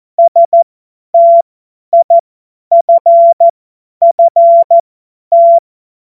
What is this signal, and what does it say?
Morse code, heard as a keyed tone, about 14 wpm, spaced as for 10 wpm, 680 Hz. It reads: STIFFT
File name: Morsekurs / Witze / Witz 02. Dezember 2025 50 F70